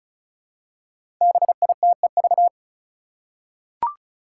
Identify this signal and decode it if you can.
Morse code, heard as a keyed tone, about 36 wpm, 700 Hz. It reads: BITEV
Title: Five Challenging Letter-Number Combinations 35wpm